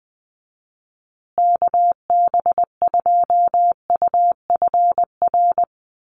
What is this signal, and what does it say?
Morse code, heard as a keyed tone, about 20 wpm, 700 Hz. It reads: KB2UFR
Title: USA Call Signs Repeated 3x 20wpm